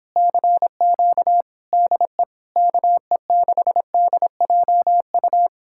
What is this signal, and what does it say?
Morse code, heard as a keyed tone, about 31 wpm, 700 Hz. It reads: CQ DE KE6DJU